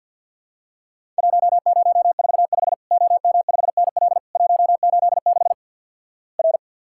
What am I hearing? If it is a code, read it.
Morse code, heard as a keyed tone, about 50 wpm, 700 Hz. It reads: 1045 OM5NL 186 R
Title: Worked All Europe DX Contest - QTC Exchange - 50wpm